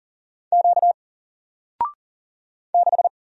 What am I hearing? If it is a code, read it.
Morse code, heard as a keyed tone, about 39 wpm, 700 Hz. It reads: Q 6
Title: Single Letter-Number Alt 40wpm